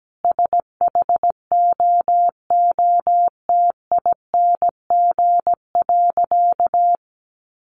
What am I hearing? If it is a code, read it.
Morse code, heard as a keyed tone, about 17 wpm, 700 Hz. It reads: SHOOTING.
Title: A Sound Of Thunder 17wpm